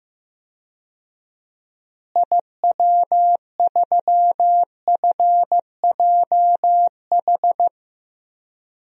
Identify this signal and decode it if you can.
Morse code, heard as a keyed tone, about 15 wpm, 700 Hz. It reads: IW3FJH
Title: DXCC-100 Call Signs - 6 Characters - Repeated 3x - 15wpm